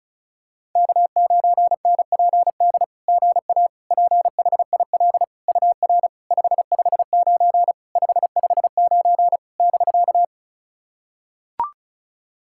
Morse code, heard as a keyed tone, about 36 wpm, 700 Hz. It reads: K9NPD GA PHIL UR 559 559 <BK>
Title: SOTA QSO Practice 35wpm